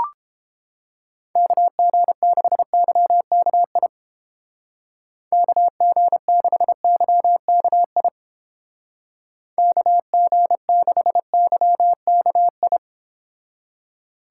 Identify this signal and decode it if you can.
Morse code, heard as a keyed tone, about 29 wpm, 700 Hz. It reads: KG6YKI KG6YKI KG6YKI